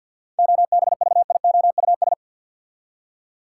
Morse code, heard as a keyed tone, about 46 wpm, 700 Hz. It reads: OBVIOUS
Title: Seven Letter Words - 50wpm